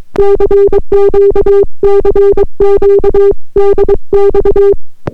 Morse code, heard as a keyed tone, about 22 wpm, 400 Hz. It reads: CQ CQ DX